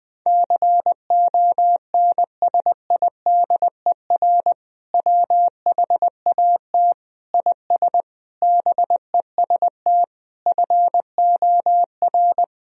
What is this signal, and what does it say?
Morse code, heard as a keyed tone, about 20 wpm, 700 Hz. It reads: CONSIDER WHAT IS BEST FOR
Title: Sentences from Top 1000 Words - Review Part 4 - 20wpm